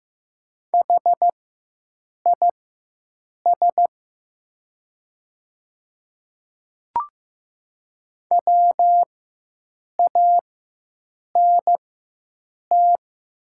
Morse code, heard as a keyed tone, about 15 wpm, spaced as for 7 wpm, 690 Hz. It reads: HIS WANT
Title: 4x Character Spacing - Top 500 Words - 15wpm